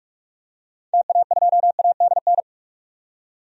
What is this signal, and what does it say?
Morse code, heard as a keyed tone, about 45 wpm, 700 Hz. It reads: TA2ADN
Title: DXCC-100 Call Signs - 6 Characters - 45wpm